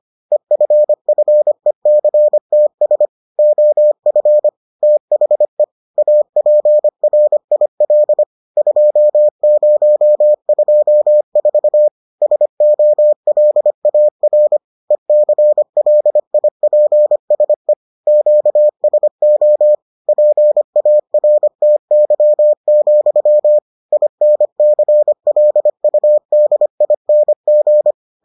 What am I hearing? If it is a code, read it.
Morse code, heard as a keyed tone, about 25 wpm, 600 Hz. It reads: EFFECTS OF THE APRIL 2024 SOLAR ECLIPSE QSO PARTY, INCLUDING